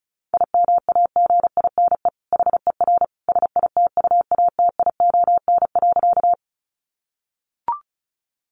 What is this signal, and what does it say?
Morse code, heard as a keyed tone, about 34 wpm, 700 Hz. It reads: IMAGINE HER SITUATION.
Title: Sentences from Top 1400 Words 35wpm